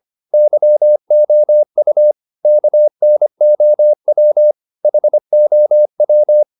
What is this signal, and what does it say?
Morse code, heard as a keyed tone, about 25 wpm, 600 Hz. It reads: YOU KNOW HOW